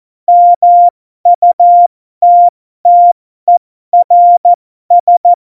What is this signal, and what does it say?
Morse code, heard as a keyed tone, about 14 wpm, spaced as for 12 wpm, 700 Hz. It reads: MUTTERS
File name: Morsekurs / Witze